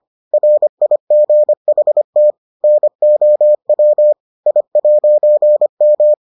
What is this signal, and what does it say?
Morse code, heard as a keyed tone, about 25 wpm, 600 Hz. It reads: RIGHT NOW I'M